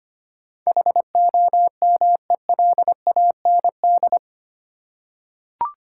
Morse code, heard as a keyed tone, about 25 wpm, 700 Hz. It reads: HOMELAND
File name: Eight Letter Words - 25wpm